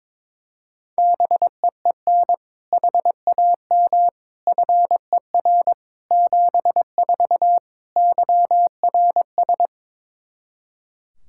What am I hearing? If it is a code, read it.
Morse code, heard as a keyed tone, about 22 wpm, 700 Hz. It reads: BEEN HAM FER 74 YRS
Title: Practice QSO - Exchange 3 - More Info - 22wpm